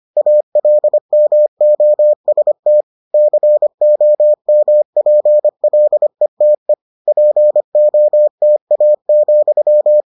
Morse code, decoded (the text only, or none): ALMOST COMPLETE POTA,